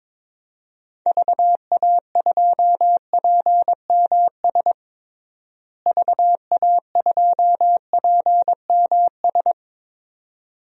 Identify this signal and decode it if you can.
Morse code, heard as a keyed tone, about 21 wpm, 700 Hz. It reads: VA2PMS VA2PMS
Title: DXCC-100 Call Signs - 6 Characters - Repeated 2x - 22wpm